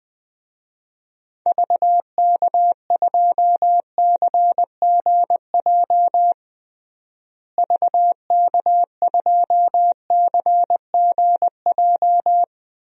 Morse code, heard as a keyed tone, about 20 wpm, 700 Hz. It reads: VK2CGJ VK2CGJ
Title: DXCC-100 Call Signs - 6 Characters - Repeated 3x - 20wpm